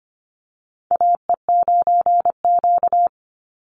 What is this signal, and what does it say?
Morse code, heard as a keyed tone, about 25 wpm, 700 Hz. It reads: AE9Q
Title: NA SOTA Chaser Callsigns 25wpm